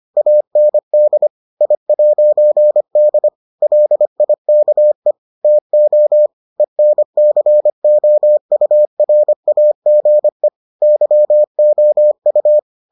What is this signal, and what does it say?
Morse code, heard as a keyed tone, about 25 wpm, 600 Hz. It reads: AND I'D LIKE TO ENCOURAGE YOU